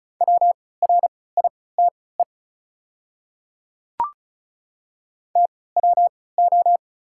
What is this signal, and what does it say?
Morse code, heard as a keyed tone, about 36 wpm, spaced as for 20 wpm, 690 Hz. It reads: WRITE TWO
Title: 3x Character Spacing - Top 100 Words - 35wpm